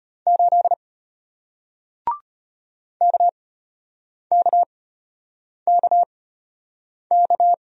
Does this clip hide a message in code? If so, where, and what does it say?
Yes: Morse code from 0.3 to 7.5 s, heard as a keyed tone, about 33 wpm, spaced as for 8 wpm, 700 Hz. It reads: 8 KKKK